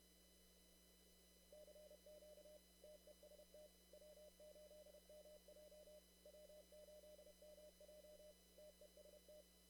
Morse code, heard as a keyed tone, about 30 wpm, 600 Hz. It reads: CQ TEST W8MJ W8MJ TEST
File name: CW / kt5j wpx cw-op-k5pi